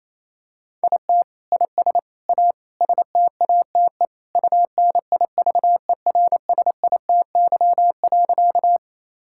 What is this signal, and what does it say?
Morse code, heard as a keyed tone, about 28 wpm, 700 Hz. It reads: IT IS A STATE UNIVERSITY.
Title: Sentences from Top 1000 Words - Review Part 5 - 28wpm